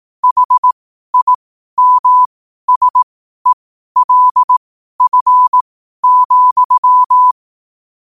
Morse code, heard as a keyed tone, about 18 wpm, spaced as for 12 wpm, 1000 Hz. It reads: HIMSELF,
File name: qotdfiles / 12wpm